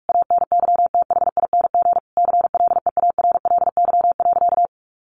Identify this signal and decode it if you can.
Morse code, heard as a keyed tone, about 44 wpm, 700 Hz. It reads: ANYTHING CLEARLY.
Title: Sentences from Top 1300 Words 45wpm